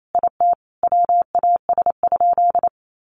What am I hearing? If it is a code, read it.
Morse code, heard as a keyed tone, about 28 wpm, 700 Hz. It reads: IT WAS?
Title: Sentences from Top 700 Words 28wpm